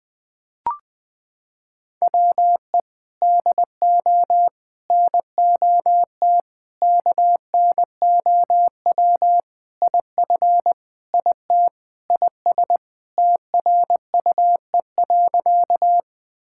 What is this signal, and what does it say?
Morse code, heard as a keyed tone, about 20 wpm, 700 Hz. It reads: WE DO NOT KNOW IF IT IS TRUE.
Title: Sentences with 5-Word Phrases - 20wpm